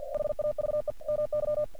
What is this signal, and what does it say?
Morse code, heard as a keyed tone, about 46 wpm, 600 Hz. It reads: HAVE MY
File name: archived / CRSnet